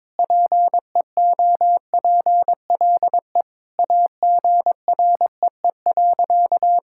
Morse code, heard as a keyed tone, about 22 wpm, 700 Hz. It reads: PEOPLE AGREE.
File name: practice / 22wpm / Sentences with 4-Word Phrases - 22wpm